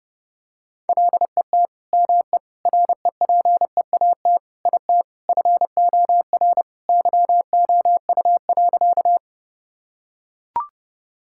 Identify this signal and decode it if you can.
Morse code, heard as a keyed tone, about 30 wpm, 700 Hz. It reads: LET ME REPEAT IT FOR YOU.
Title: Sentences from Top 1600 Words 30wpm